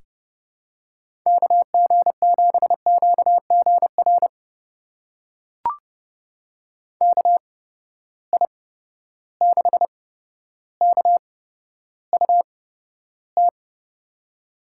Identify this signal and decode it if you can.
Morse code, heard as a keyed tone, about 30 wpm, spaced as for 13 wpm, 700 Hz. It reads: KG7QGR KI6KUT
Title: USA Call Signs - 6 Characters - 8x Character Spacing - 30wpm